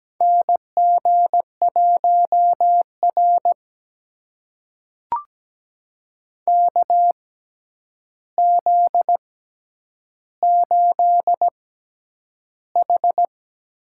Morse code, heard as a keyed tone, about 17 wpm, spaced as for 9 wpm, 700 Hz. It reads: NG1R KZ8H
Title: USA Call Signs - 4 Characters - 6x Character Spacing - 17wpm